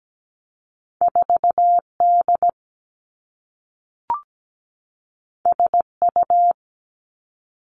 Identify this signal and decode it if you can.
Morse code, heard as a keyed tone, about 17 wpm, 700 Hz. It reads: 4D SU
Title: Two Challenging Letter-Number Combinations 17wpm